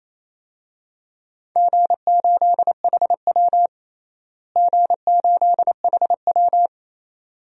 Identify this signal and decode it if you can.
Morse code, heard as a keyed tone, about 28 wpm, 700 Hz. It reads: G8HW G8HW
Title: DXCC-100 Call Signs - 4 Characters - Repeated 2x - 28wpm